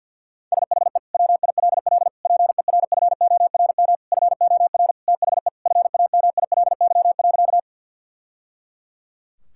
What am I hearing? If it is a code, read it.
Morse code, heard as a keyed tone, about 51 wpm, 700 Hz. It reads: SHE WILL PERFORM FOR THE FAMILY.
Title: Sentences from Top 1000 Words - Review Part 5 - 50wpm